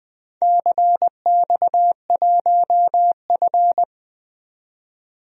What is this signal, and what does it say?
Morse code, heard as a keyed tone, about 20 wpm, 700 Hz. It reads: CX1F
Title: DXCC-100 Call Signs - 4 Characters - 20wpm